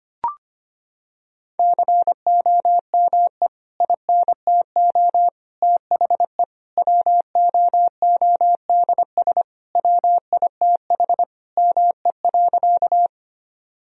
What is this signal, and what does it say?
Morse code, heard as a keyed tone, about 25 wpm, 700 Hz. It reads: COME INTO THE WOODS WITH ME.
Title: Sentences from Top 1000 Words - Review Part 2 - 25wpm